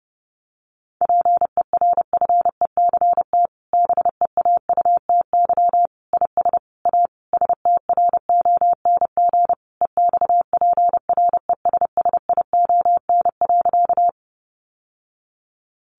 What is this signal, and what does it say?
Morse code, heard as a keyed tone, about 30 wpm, 700 Hz. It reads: PERFECT BEAUTY IS A STRONG EXPRESSION.
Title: Sentences from Top 1200 Words 30wpm